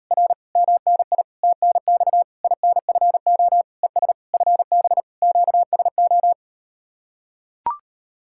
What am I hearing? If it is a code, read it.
Morse code, heard as a keyed tone, about 39 wpm, 700 Hz. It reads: R MNI TNX INFO ES FB QSO